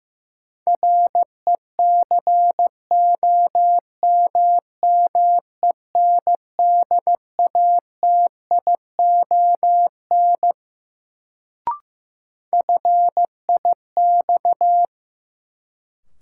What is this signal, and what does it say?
Morse code, heard as a keyed tone, about 16 wpm, 700 Hz. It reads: RECOMMENDATION FIX
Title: Sentences from Top 1400 Words - New Word Drill Alt - 15wpm